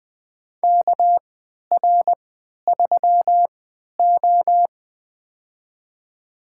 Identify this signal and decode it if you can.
Morse code, heard as a keyed tone, about 21 wpm, spaced as for 12 wpm, 700 Hz. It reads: KR3O
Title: USA Call Signs - 4 Characters - 3x Character Spacing - 20wpm